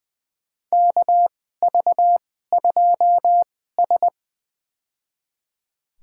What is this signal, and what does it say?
Morse code, heard as a keyed tone, about 20 wpm, spaced as for 14 wpm, 700 Hz. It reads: KV2S
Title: USA Call Signs - 4 Characters - 2x Character Spacing - 20wpm